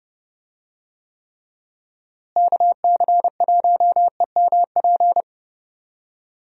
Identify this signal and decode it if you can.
Morse code, heard as a keyed tone, about 31 wpm, 700 Hz. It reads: KC1EMP
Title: USA Call Signs - 6 Characters - Repeated 2x - 30wpm